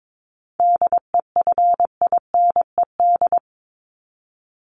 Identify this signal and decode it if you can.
Morse code, heard as a keyed tone, about 22 wpm, 700 Hz. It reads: DEFINED
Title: ED Suffix - Top 100 Words with Base Words 22wpm